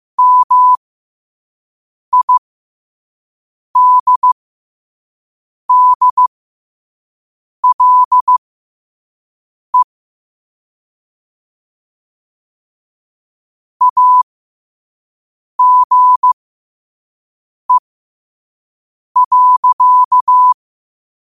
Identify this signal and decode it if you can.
Morse code, heard as a keyed tone, about 15 wpm, spaced as for 5 wpm, 1000 Hz. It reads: MIDDLE AGE.
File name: qotdfiles / 5wpm